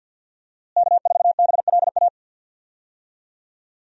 Morse code, heard as a keyed tone, about 52 wpm, 700 Hz. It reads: K4BLA
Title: USA Call Signs - 5 Characters - Repeated 3x - 50wpm